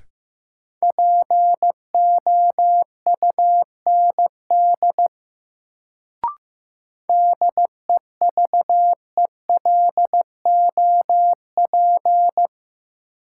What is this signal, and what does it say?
Morse code, heard as a keyed tone, about 15 wpm, 700 Hz. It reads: POUND DEVELOP
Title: Sentences from Top 1000 Words - Review Part 2 - New Word Drill - 15wpm